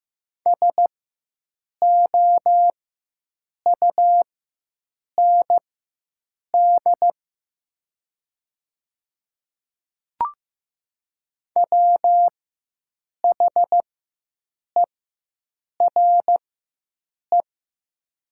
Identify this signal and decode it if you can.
Morse code, heard as a keyed tone, about 15 wpm, spaced as for 7 wpm, 700 Hz. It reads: SOUND WHERE
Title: 4x Character Spacing - Top 200 Words - 15wpm